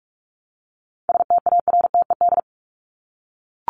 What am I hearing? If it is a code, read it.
Morse code, heard as a keyed tone, about 44 wpm, 700 Hz. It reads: STARTED
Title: ED Suffix - Top 100 Words with Base Words 45wpm